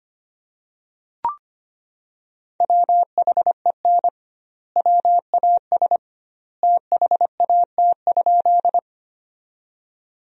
Morse code, heard as a keyed tone, about 25 wpm, 700 Hz. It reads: WHEN WAS THAT?
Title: 2x Word-Spacing - Sentences from Top 100 Words - 25wpm